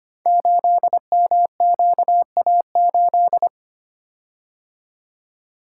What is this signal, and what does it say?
Morse code, heard as a keyed tone, about 25 wpm, 700 Hz. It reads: 8MQA8
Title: Code Groups - 5 Letter:Numbers - 25wpm